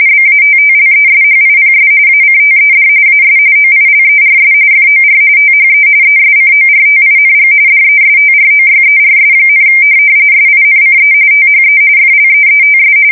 RTTY, about 45.45 baud. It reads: BATTLESHIPS ACTED AS OILERS FOR THE ESCORTING DESTROYERS, SINCE THE FLEETS
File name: fsk170 / USS-Missouri